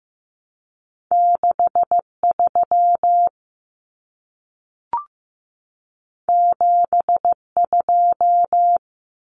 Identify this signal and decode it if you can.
Morse code, heard as a keyed tone, about 15 wpm, 700 Hz. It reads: 63 72
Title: Common Cut Numbers - 2-Digit 15wpm